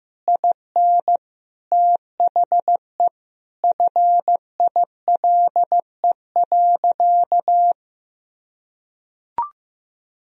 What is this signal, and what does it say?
Morse code, heard as a keyed tone, about 16 wpm, 700 Hz. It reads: IN THE FILE.
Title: Sentences from Top 1000 Words - Review Part 4 - 15wpm